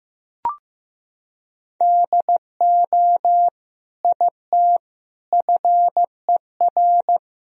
Morse code, heard as a keyed tone, about 15 wpm, 700 Hz. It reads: DO IT FER
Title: Sentences from Top 200 Words and Abbreviations 15wpm